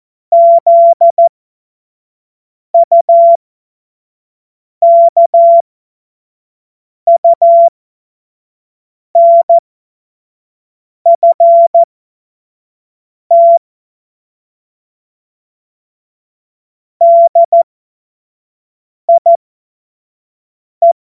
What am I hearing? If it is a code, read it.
Morse code, heard as a keyed tone, about 14 wpm, spaced as for 5 wpm, 680 Hz. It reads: ZUKUNFT DIE
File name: Morsekurs / Witze / Witz 04. Januar 2022, 25 F70